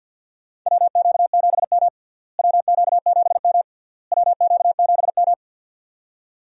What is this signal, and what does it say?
Morse code, heard as a keyed tone, about 52 wpm, 700 Hz. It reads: WQ7M WQ7M WQ7M
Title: USA Call Signs - 4 Characters - Repeated 3x - 50wpm